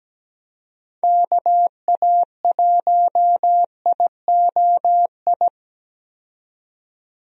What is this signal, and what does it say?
Morse code, heard as a keyed tone, about 17 wpm, 700 Hz. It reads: KA1IOI
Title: USA Call Signs - 6 Characters - 17wpm